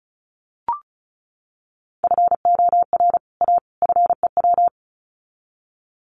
Morse code, heard as a keyed tone, about 34 wpm, 700 Hz. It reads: FOR A FEW
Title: Sets of 3 Words 35wpm